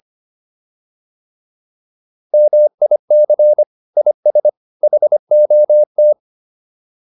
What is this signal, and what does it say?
Morse code, heard as a keyed tone, about 25 wpm, 600 Hz. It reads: MIC IS HOT